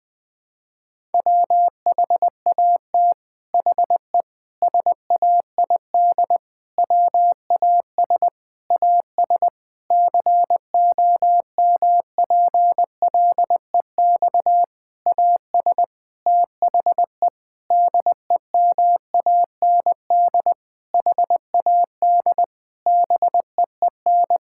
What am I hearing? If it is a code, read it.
Morse code, heard as a keyed tone, about 20 wpm, 700 Hz. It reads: WHAT HE SAID WAS AS COMPLEX AS THE DEMAND HAD BEEN
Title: Sentences from Top 1000 Words - Review Part 6 - 20wpm